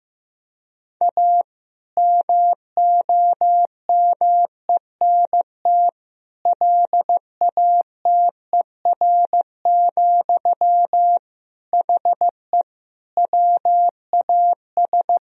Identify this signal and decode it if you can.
Morse code, heard as a keyed tone, about 15 wpm, 700 Hz. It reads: A MOMENT LATER, HE WAS
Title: Sentences from Top 1000 Words - Review Part 4 - 15wpm